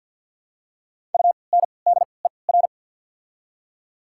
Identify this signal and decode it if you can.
Morse code, heard as a keyed tone, about 52 wpm, spaced as for 29 wpm, 690 Hz. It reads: UNDER
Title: 3x Character Spacing - Top 300 Words - 50wpm